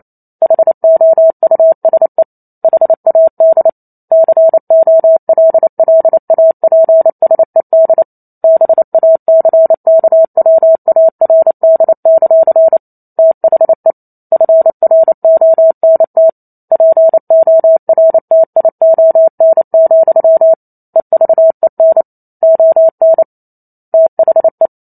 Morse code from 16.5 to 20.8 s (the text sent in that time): PORTION,